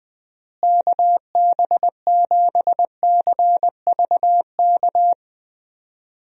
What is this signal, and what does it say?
Morse code, heard as a keyed tone, about 20 wpm, 700 Hz. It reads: KB7CVK
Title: USA Call Signs - 6 Characters - Repeated 2x - 20wpm